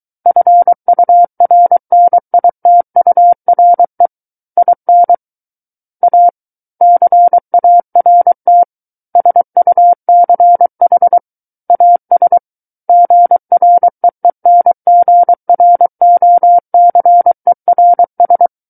Morse code, heard as a keyed tone, about 23 wpm, 700 Hz. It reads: FURNITURE IN A CART SUCH AS GREENGROCERS
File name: Book One / 14-Chapter 14 - 23 WPM